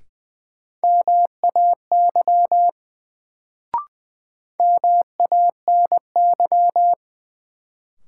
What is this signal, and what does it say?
Morse code, heard as a keyed tone, about 20 wpm, 700 Hz. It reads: MAY MANY
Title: Sets of 2 Words - New Word Drill - 20wpm